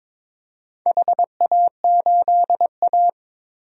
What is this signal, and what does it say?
Morse code, heard as a keyed tone, about 23 wpm, 700 Hz. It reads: HA8A
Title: DXCC-100 Call Signs - 4 Characters - 22wpm